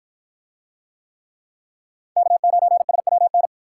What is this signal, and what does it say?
Morse code, heard as a keyed tone, about 53 wpm, 700 Hz. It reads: K9SWN